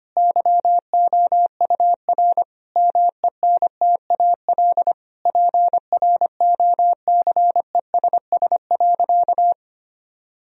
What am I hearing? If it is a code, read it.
Morse code, heard as a keyed tone, about 25 wpm, 700 Hz. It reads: YOUR MENTAL PROCESS.
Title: Sentences from Top 1000 Words - Review Part 4 - 25wpm